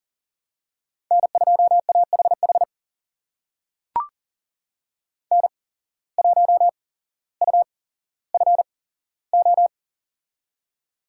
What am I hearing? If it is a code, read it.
Morse code, heard as a keyed tone, about 38 wpm, spaced as for 17 wpm, 700 Hz. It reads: N2AHH N1UFO
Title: USA Call Signs - 5 Characters - 8x Character Spacing - 40wpm